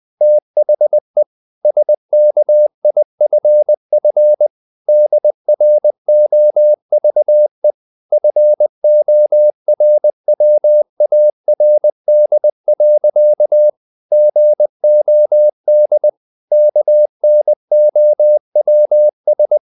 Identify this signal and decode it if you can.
Morse code, heard as a keyed tone, about 20 wpm, 600 Hz. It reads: THE SKIFF DROVE FORWARD. GOD KNOWS
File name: the old man and the sea part 2 20wpm